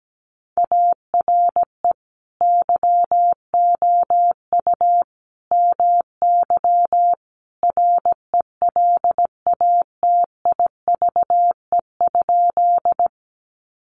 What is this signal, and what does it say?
Morse code, heard as a keyed tone, about 17 wpm, 700 Hz. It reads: ARE YOU MY RELATIVE?
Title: Sentences from Top 1400 Words 17wpm